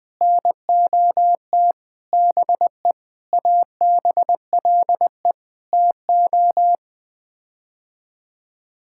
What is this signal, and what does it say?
Morse code, heard as a keyed tone, about 20 wpm, 700 Hz. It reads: NOT BE ABLE TO
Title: Sets of 6 Words - 20wpm